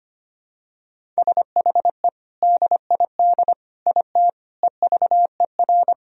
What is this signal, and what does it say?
Morse code, heard as a keyed tone, about 25 wpm, 700 Hz. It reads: SHE DID IT EVER
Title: Sentences from Top 400 Words and Abbreviations 25wpm